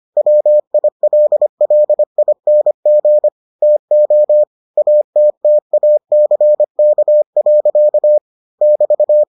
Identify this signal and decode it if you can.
Morse code, heard as a keyed tone, about 25 wpm, 600 Hz. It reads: WILLING TO ATTACK. <BT>